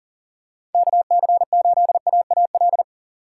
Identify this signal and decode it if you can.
Morse code, heard as a keyed tone, about 40 wpm, 700 Hz. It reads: KC8AAL
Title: USA Call Signs - 6 Characters - 2x Character Spacing - 40wpm